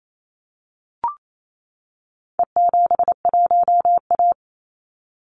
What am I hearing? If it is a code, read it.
Morse code, heard as a keyed tone, about 28 wpm, 700 Hz. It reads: E71A
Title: Call Signs - 28wpm